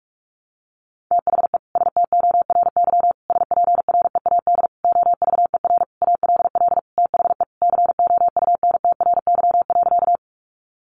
Morse code, heard as a keyed tone, about 44 wpm, 700 Hz. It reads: THE STORY SPREAD OVER ALL THE COUNTRY.
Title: Sentences from Top 1200 Words 45wpm